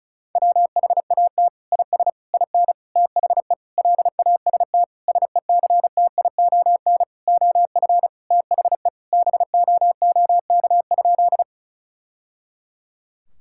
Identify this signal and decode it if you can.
Morse code, heard as a keyed tone, about 36 wpm, 700 Hz. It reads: WHAT IS IN THE LAST SECTION OF THE BOOK?
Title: Sentences from Top 1000 Words - Review Part 5 - 35wpm